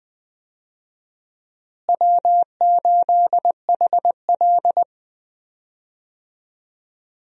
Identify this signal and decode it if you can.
Morse code, heard as a keyed tone, about 20 wpm, 700 Hz. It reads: W8HL
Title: USA Call Signs - 4 Characters - Repeated 2x - 20wpm